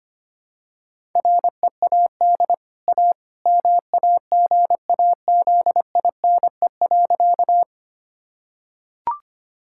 Morse code, heard as a keyed tone, about 25 wpm, 700 Hz. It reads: READ A MAGAZINE.
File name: Easy Sentences with Hints - 25wpm